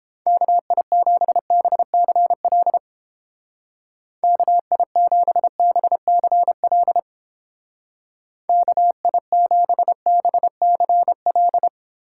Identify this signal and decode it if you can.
Morse code, heard as a keyed tone, about 31 wpm, 700 Hz. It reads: KI7BCL KI7BCL KI7BCL